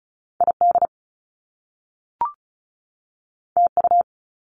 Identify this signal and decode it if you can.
Morse code, heard as a keyed tone, about 36 wpm, 700 Hz. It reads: ID TU
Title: Top Two Letter Combinations in Words 35wpm